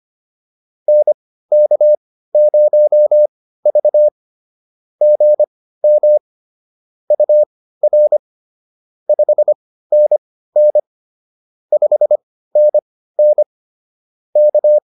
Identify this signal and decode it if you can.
Morse code, heard as a keyed tone, about 25 wpm, spaced as for 15 wpm, 600 Hz. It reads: NK0V GM UR 5NN 5NN K
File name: POTA101 15